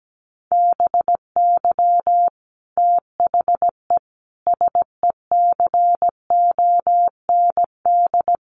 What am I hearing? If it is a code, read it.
Morse code, heard as a keyed tone, about 17 wpm, 700 Hz. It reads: BY THE SECOND
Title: Sentences from Top 1200 Words 17wpm